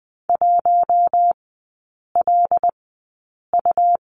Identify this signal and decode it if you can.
Morse code, heard as a keyed tone, about 20 wpm, spaced as for 8 wpm, 700 Hz. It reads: 1LU
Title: 107 Lesson 18e - Letter m - Three Characters 20wpm